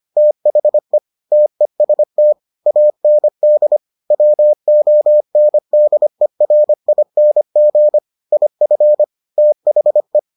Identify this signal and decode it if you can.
Morse code, heard as a keyed tone, about 25 wpm, 600 Hz. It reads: THE TEST AND WONDERING IF THE